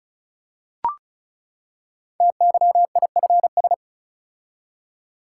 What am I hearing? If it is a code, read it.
Morse code, heard as a keyed tone, about 36 wpm, 700 Hz. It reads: TYIFS
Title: Code Groups - 5 Letter:Numbers - 35wpm